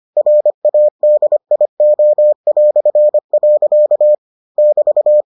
Morse code, heard as a keyed tone, about 25 wpm, 600 Hz. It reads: RADIO". <BT>